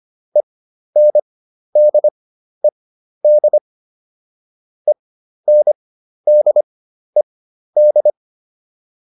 Morse code, heard as a keyed tone, about 25 wpm, spaced as for 12 wpm, 600 Hz. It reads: ENDED ENDED